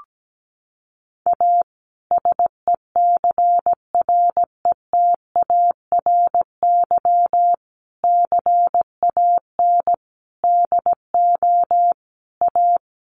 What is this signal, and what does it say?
Morse code, heard as a keyed tone, about 17 wpm, 700 Hz. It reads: A SECRETARY CAN DO A
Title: Sentences from Top 1300 Words 17wpm